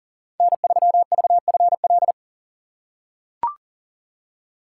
Morse code, heard as a keyed tone, about 40 wpm, 700 Hz. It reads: N3VFL
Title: USA Call Signs - 5 Characters - 4x Character Spacing - 40wpm